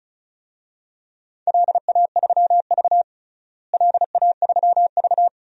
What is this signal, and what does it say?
Morse code, heard as a keyed tone, about 36 wpm, 700 Hz. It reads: LA3V LA3V
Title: DXCC-100 Call Signs - 4 Characters - Repeated 2x - 35wpm